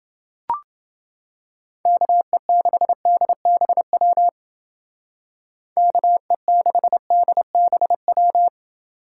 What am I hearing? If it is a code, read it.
Morse code, heard as a keyed tone, about 29 wpm, 700 Hz. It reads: KE6DBW KE6DBW